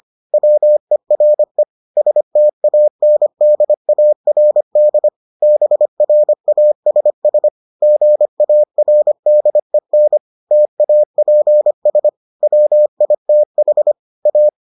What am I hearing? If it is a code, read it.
Morse code, heard as a keyed tone, about 25 wpm, 600 Hz. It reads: WERE STANDARD BRASS GARDEN TAPS WITH A